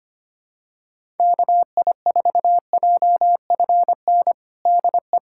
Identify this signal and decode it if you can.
Morse code, heard as a keyed tone, about 25 wpm, 700 Hz. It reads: KI4JFN DE
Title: USA Callsign DE Callsign - 6 Characters - 25wpm